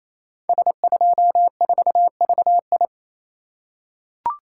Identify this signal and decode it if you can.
Morse code, heard as a keyed tone, about 28 wpm, 700 Hz. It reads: S24VI
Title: Code Groups - 5 Letter:Numbers - 28wpm